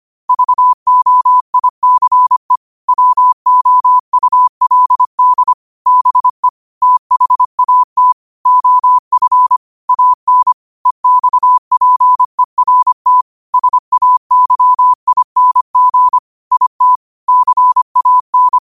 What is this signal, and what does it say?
Morse code, heard as a keyed tone, about 25 wpm, 1000 Hz. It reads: VOICE WOULD BE THAT OF AN EXPERT SAYING IT CAN